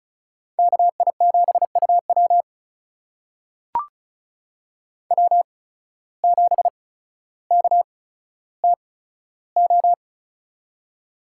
Morse code, heard as a keyed tone, about 36 wpm, spaced as for 15 wpm, 700 Hz. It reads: KI7UW W7KTO
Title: USA Call Signs - 5 Characters - 8x Character Spacing - 35wpm